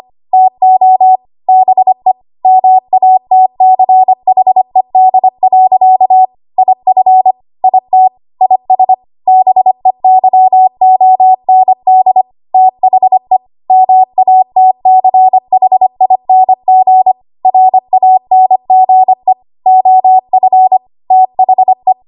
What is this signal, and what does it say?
Morse code, heard as a keyed tone, about 25 wpm, 750 Hz. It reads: TO BE MATCHED. IF IT IS BEYOND THE MATCHING RANGE OF THE
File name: Archive / 25 WPM / 25WPM